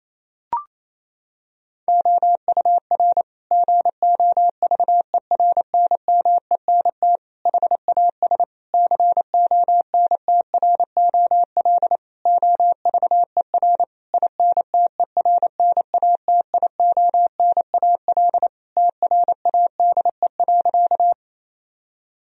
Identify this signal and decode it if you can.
Morse code, heard as a keyed tone, about 28 wpm, 700 Hz. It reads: OUR GOVERNMENT HAS CONTROL OVER INTERNATIONAL TRADE.
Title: Sentences from Top 1000 Words - Review Part 4 - 28wpm